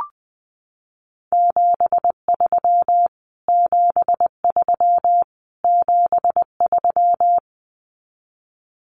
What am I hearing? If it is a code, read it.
Morse code, heard as a keyed tone, about 20 wpm, 700 Hz. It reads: 73 73 73
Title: Abbreviations Repeated 3x - 20wpm